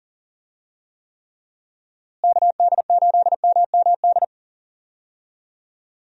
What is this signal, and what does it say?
Morse code, heard as a keyed tone, about 40 wpm, 700 Hz. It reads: KD8MMD
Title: USA Call Signs - 6 Characters - Repeated 3x - 40wpm